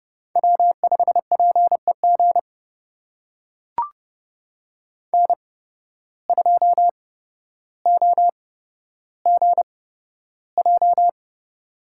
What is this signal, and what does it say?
Morse code, heard as a keyed tone, about 30 wpm, spaced as for 13 wpm, 700 Hz. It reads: W5PEG N2OGJ
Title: USA Call Signs - 5 Characters - 8x Character Spacing - 30wpm